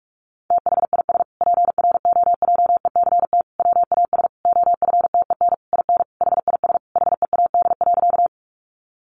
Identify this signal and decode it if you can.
Morse code, heard as a keyed tone, about 44 wpm, 700 Hz. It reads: THIS PROJECT WAS OFTEN IN HIS HEAD.
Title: Sentences from Top 1400 Words 45wpm